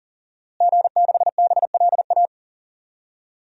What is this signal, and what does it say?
Morse code, heard as a keyed tone, about 40 wpm, 700 Hz. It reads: G6BLA
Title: DXCC-100 Call Signs - 5 Characters - Repeated 2x - 40wpm